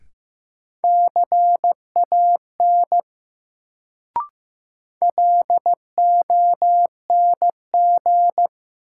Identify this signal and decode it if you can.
Morse code, heard as a keyed tone, about 15 wpm, 700 Hz. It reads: CAN LONG
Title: Sentences from Top 100 Words - New Word Drill - 15wpm